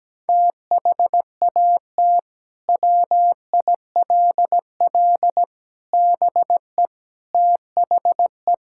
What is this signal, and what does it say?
Morse code, heard as a keyed tone, about 17 wpm, 700 Hz. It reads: THAT WILL BE THE
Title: Sentences from Top 1000 Words - Review Part 6 - 17wpm